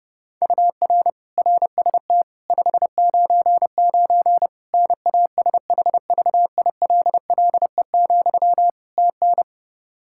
Morse code, heard as a keyed tone, about 30 wpm, 700 Hz. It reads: UR RST 599 NASHVILLE, TN
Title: Practice QSO - Exchange 1 - Beginning - 30wpm